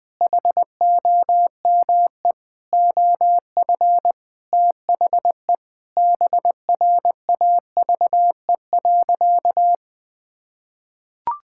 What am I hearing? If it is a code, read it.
Morse code, heard as a keyed tone, about 21 wpm, 700 Hz. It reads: HOME OF THE BRAVE.
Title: Sentences with 9-Word Phrases - Spoken Beginning - 20wpm